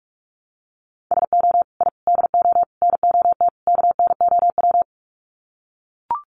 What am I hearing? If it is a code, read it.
Morse code, heard as a keyed tone, about 44 wpm, 700 Hz. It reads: SO I DO NOT KNOW
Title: Sets of 5 Words Encore 45wpm